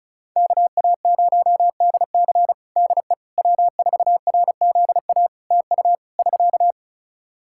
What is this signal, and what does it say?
Morse code, heard as a keyed tone, about 36 wpm, 700 Hz. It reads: KA0DC DE W4RZA TU <SK>
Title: Practice QSO - Exchange 4 - Ending - 35wpm